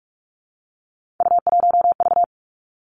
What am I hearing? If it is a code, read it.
Morse code, heard as a keyed tone, about 46 wpm, 700 Hz. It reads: U1V
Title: Three Challenging Letter-Number Combinations 45wpm